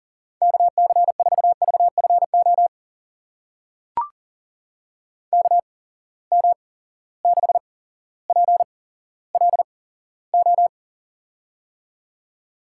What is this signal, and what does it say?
Morse code, heard as a keyed tone, about 40 wpm, spaced as for 17 wpm, 700 Hz. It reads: KC4VFO KM6PLO
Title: USA Call Signs - 8x Character Spacing - 40wpm